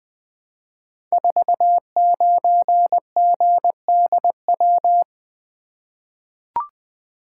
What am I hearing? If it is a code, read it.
Morse code, heard as a keyed tone, about 20 wpm, 700 Hz. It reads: 49GDW
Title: Code Groups - 5 Letter:Numbers - 20wpm